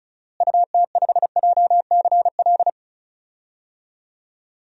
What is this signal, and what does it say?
Morse code, heard as a keyed tone, about 36 wpm, 700 Hz. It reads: UT5JCL
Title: DXCC-100 Call Signs - 6 Characters - Repeated 2x - 35wpm